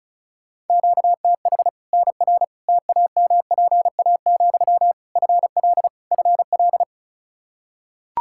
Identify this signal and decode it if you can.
Morse code, heard as a keyed tone, about 36 wpm, 700 Hz. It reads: QTH NR TAMPA, FL FL
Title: K1USN SST QSO Practice 35wpm